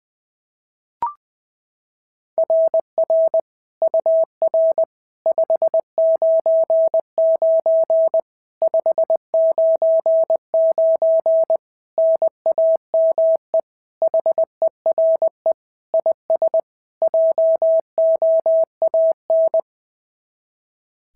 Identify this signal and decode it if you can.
Morse code, heard as a keyed tone, about 20 wpm, 690 Hz. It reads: RR UR 599 599 NAME HERE IS JOAN
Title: SOTA QSO Practice 20wpm